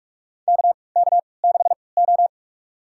Morse code, heard as a keyed tone, about 46 wpm, spaced as for 26 wpm, 700 Hz. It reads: KK6O
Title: USA Call Signs - 3x Character Spacing - 45wpm